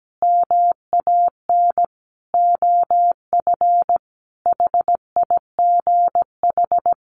Morse code, heard as a keyed tone, about 17 wpm, 700 Hz. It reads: MAN OF HIGH
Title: Sentences from Top 1200 Words 17wpm